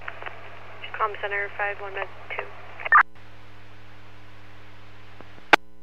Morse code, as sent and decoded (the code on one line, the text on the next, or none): none